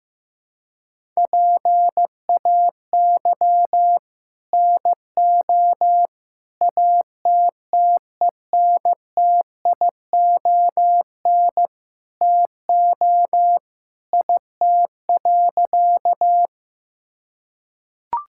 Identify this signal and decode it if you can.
Morse code, heard as a keyed tone, about 15 wpm, 700 Hz. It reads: PAY NO ATTENTION TO IT.
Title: Sentences from Top 1000 Words - Review Part 6 - 15wpm